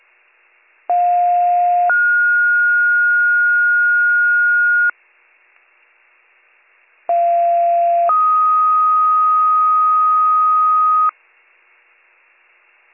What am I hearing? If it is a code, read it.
Two-tone page, heard as a one-second tone, then a three-second tone, about 710 then 1430 Hz, about 690 then 1250 Hz. It is heard from 0.9 to 11.1 s.